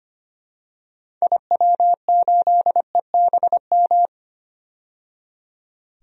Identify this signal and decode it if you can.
Morse code, heard as a keyed tone, about 25 wpm, 700 Hz. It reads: IW8EBM
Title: DXCC-100 Call Signs - 6 Characters - 25wpm